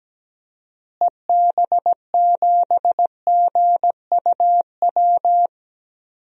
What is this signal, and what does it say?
Morse code, heard as a keyed tone, about 17 wpm, 700 Hz. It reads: EB7GUW
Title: DXCC-100 Call Signs - 6 Characters - Repeated 2x - 17wpm